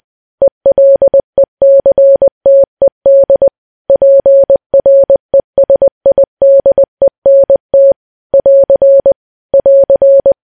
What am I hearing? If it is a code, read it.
Morse code, heard as a keyed tone, about 20 wpm, 550 Hz. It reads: ELECTED PRESIDENT <AR> <AR>
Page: news-headlines-in-morse-code-at-20-wpm